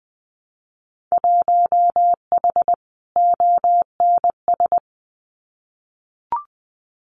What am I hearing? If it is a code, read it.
Morse code, heard as a keyed tone, about 20 wpm, 700 Hz. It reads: 1H ONS
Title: Winter Field Day Exchanges Encore 20wpm